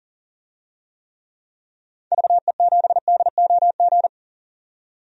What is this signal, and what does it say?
Morse code, heard as a keyed tone, about 40 wpm, 700 Hz. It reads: VE7DOG
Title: DXCC-100 Call Signs - 6 Characters - 40wpm